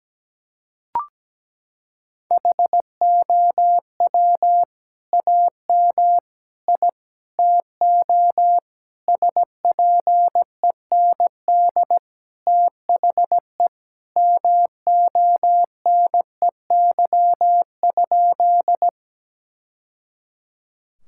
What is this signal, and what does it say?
Morse code, heard as a keyed tone, about 17 wpm, 700 Hz. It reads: HOW AM I TO SPEND THE MONEY?
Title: Sentences from Top 1000 Words - Review Part 5 - 17wpm